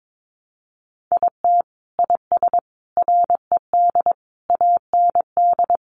Morse code, heard as a keyed tone, about 22 wpm, 700 Hz. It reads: IT IS RED AND
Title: Easy Phrases 22wpm